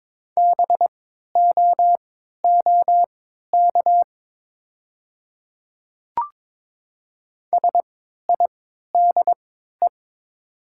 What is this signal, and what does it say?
Morse code, heard as a keyed tone, about 22 wpm, spaced as for 12 wpm, 700 Hz. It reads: BOOK SIDE
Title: 3x Character Spacing - Top 300 Words - 22wpm